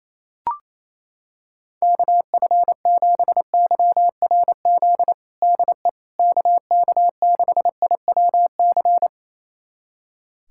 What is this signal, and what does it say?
Morse code, heard as a keyed tone, about 28 wpm, 700 Hz. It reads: KF7YRZ DE KK6IWC
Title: USA Callsign DE Callsign - 6 Characters - 28wpm